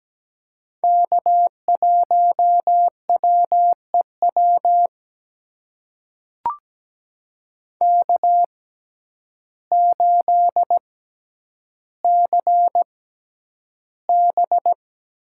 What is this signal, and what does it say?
Morse code, heard as a keyed tone, about 17 wpm, spaced as for 9 wpm, 700 Hz. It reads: K1WEW K8CB
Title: USA Call Signs - 5 Characters - 6x Character Spacing - 17wpm